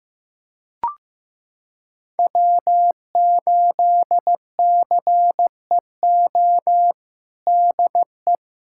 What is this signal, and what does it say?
Morse code, heard as a keyed tone, about 16 wpm, 700 Hz. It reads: W8CEO DE
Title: USA Callsign DE Callsign 15wpm